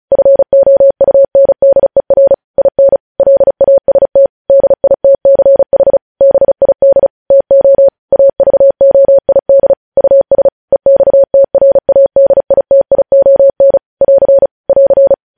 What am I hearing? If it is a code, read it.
Morse code, heard as a keyed tone, about 36 wpm, 550 Hz. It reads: FOUNDER IN LAST DITCH BID TO AVOID US EXTRADITION <AR> <AR>
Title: News Headlines at 35